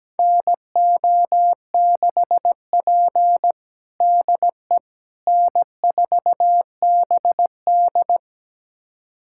Morse code, heard as a keyed tone, about 17 wpm, 700 Hz. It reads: NO6P DE N4BD
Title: USA Callsign DE Callsign - 4 Characters - 17wpm